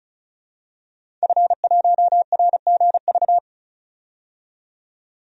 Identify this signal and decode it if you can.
Morse code, heard as a keyed tone, about 36 wpm, 700 Hz. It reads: F1RGV
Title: DXCC-100 Call Signs - 5 Characters - 35wpm